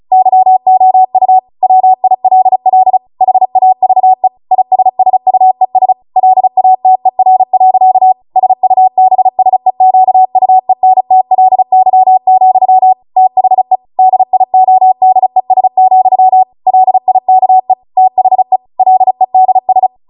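Morse code, heard as a keyed tone, about 36 wpm, 750 Hz. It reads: YOU WILL HAVE ISSUES LATER. SUBSEQUENTLY, THE DIODES, LIKE THE LEDS
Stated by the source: Amateur radio Morse code practice archive transmission at 35 wpm.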